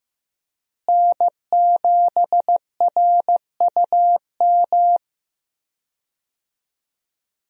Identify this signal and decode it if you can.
Morse code, heard as a keyed tone, about 15 wpm, 700 Hz. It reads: N7RUM
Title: USA Call Signs - 5 Characters - Repeated 2x - 15wpm